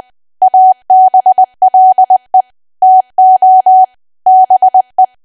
Morse code, heard as a keyed tone, about 20 wpm, 750 Hz. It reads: ABLE TO BE
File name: cwa / W1AW-Level3 / 20WPM